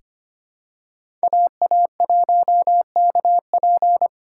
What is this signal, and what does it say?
Morse code, heard as a keyed tone, about 25 wpm, 700 Hz. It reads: AA1KP
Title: USA Call Signs - 5 Characters - 4x Character Spacing - 25wpm